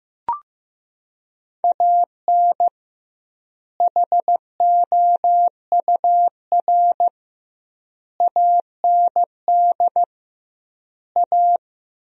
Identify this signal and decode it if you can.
Morse code, heard as a keyed tone, about 15 wpm, 700 Hz. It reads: AN HOUR AND A
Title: 2x Word-Spacing - Sentences from Top 400 Words - 15wpm